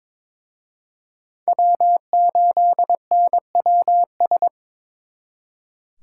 Morse code, heard as a keyed tone, about 26 wpm, 700 Hz. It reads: W8NWS
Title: Speed Racing - USA 1x3 Call Signs 22wpm